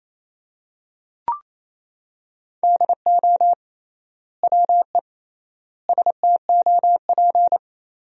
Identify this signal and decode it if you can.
Morse code, heard as a keyed tone, about 28 wpm, 700 Hz. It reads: DO WE STOP
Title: 3x Word-Spacing - Sentences from Top 300 Words - 28wpm